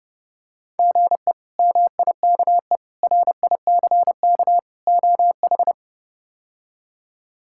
Morse code, heard as a keyed tone, about 32 wpm, 700 Hz. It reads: GE MIKE RICK OH
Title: K1USN SST QSO Practice - 30wpm